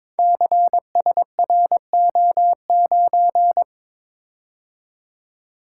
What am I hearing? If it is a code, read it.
Morse code, heard as a keyed tone, about 22 wpm, 700 Hz. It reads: CSRO9
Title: Code Groups - 5 Letter:Numbers - 22wpm